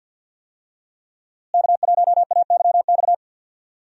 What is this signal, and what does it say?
Morse code, heard as a keyed tone, about 53 wpm, 700 Hz. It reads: K1AYX
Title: USA Call Signs - 5 Characters - Repeated 3x - 50wpm